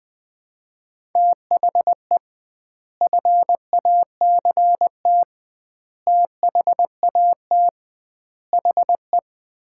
Morse code, heard as a keyed tone, about 20 wpm, 700 Hz. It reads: THE FACT THAT HE
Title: 2x Word-Spacing - Sets of 5 Words - 20wpm